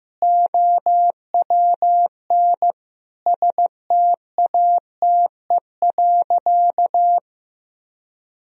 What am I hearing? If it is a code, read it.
Morse code, heard as a keyed tone, about 16 wpm, 700 Hz. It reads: OWN STATE.
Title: Sentences from Top 1000 Words - Review Part 3 - 15wpm